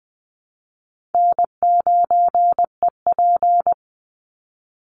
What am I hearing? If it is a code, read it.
Morse code, heard as a keyed tone, about 20 wpm, 700 Hz. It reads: N9EP
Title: Easy Call Signs Repeated 3x - 20wpm